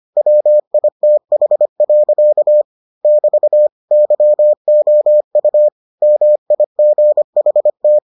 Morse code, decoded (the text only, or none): WITH. <BT> YOU MIGHT